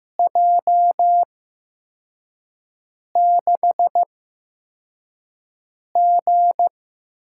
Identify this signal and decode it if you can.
Morse code, heard as a keyed tone, about 15 wpm, spaced as for 4 wpm, 700 Hz. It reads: J6G
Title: USA Call Signs - 5 Characters - 8x Character Spacing - 15wpm